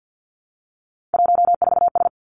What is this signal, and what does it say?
Morse code, heard as a keyed tone, about 50 wpm, 700 Hz. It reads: 14S
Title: Three Challenging Letter-Number Combinations 50wpm